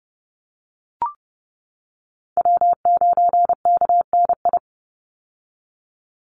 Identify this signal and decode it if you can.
Morse code, heard as a keyed tone, about 30 wpm, 700 Hz. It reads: W9KNI
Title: Call Signs - 30wpm